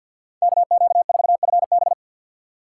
Morse code, heard as a keyed tone, about 49 wpm, 700 Hz. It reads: KQ4FB
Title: USA Call Signs - 5 Characters - 2x Character Spacing - 50wpm